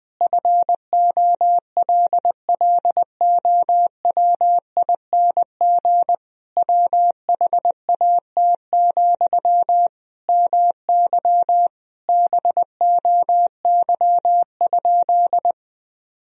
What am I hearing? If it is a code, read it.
Morse code, heard as a keyed tone, about 20 wpm, 700 Hz. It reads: FOLLOWING WHAT, MY BOY?
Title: Sentences from Top 1000 Words - Review Part 4 - 20wpm